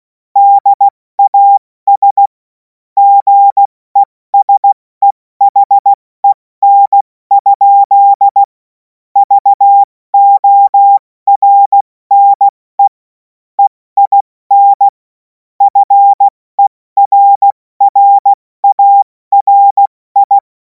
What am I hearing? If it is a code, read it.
Morse code, heard as a keyed tone, about 16 wpm, 800 Hz. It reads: DAS GESEHEN? VORNE EIN FERRARI
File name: Morsekurs / Witze / Witz 05. Mai 2026 70 F80